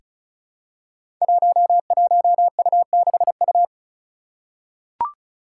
Morse code, heard as a keyed tone, about 36 wpm, 700 Hz. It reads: 11U6U
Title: Five Challenging Letter-Number Combinations 35wpm